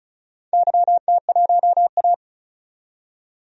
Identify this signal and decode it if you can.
Morse code, heard as a keyed tone, about 36 wpm, 700 Hz. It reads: YT1A
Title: DXCC-100 Call Signs - 4 Characters - 35wpm